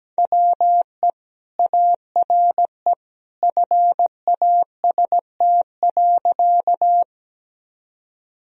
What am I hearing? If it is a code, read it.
Morse code, heard as a keyed tone, about 17 wpm, 700 Hz. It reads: WE ARE FAST.
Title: Sentences from Top 1000 Words - Review Part 2 - 17wpm